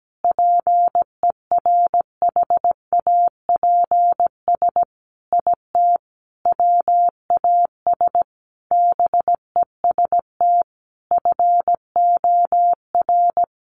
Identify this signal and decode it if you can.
Morse code, heard as a keyed tone, about 17 wpm, 700 Hz. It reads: PERHAPS IT WAS BEST FOR
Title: Sentences from Top 500 Words 17wpm